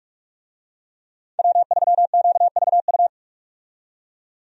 Morse code, heard as a keyed tone, about 46 wpm, 700 Hz. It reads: W3QVU
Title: USA Call Signs - 5 Characters - 45wpm